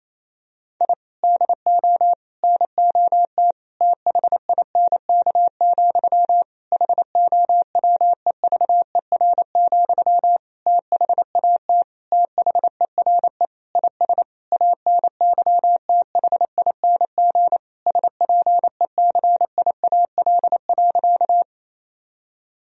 Morse code, heard as a keyed tone, about 27 wpm, 700 Hz. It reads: I DO NOT THINK, HOWEVER, THAT THERE IS ANYTHING SPECIAL.
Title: Sentences from Top 1000 Words - Review Part 3 - 28wpm